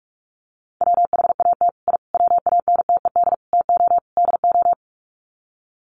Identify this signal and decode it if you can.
Morse code, heard as a keyed tone, about 44 wpm, 700 Hz. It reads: WHAT I WANTED TO DO
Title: Sets of 5 Words Encore2 45wpm